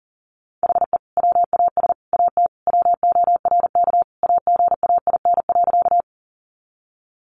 Morse code, heard as a keyed tone, about 40 wpm, 700 Hz. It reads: HE WAS AT WORK AGAIN.
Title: Sentences from Top 200 Words 40wpm